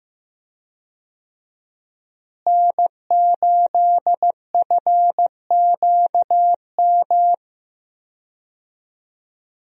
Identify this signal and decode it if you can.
Morse code, heard as a keyed tone, about 15 wpm, 700 Hz. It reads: N8FQM
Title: USA Call Signs - 5 Characters - Repeated 2x - 15wpm